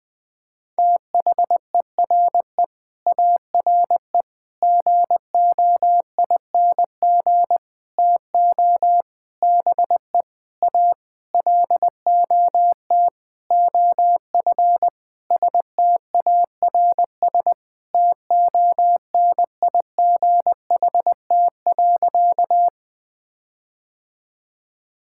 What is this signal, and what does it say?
Morse code, heard as a keyed tone, about 20 wpm, 700 Hz. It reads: THERE ARE GOING TO BE A LOT OF STARS TONIGHT.
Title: Sentences with 8-Word Phrases - Spoken Beginning - 20wpm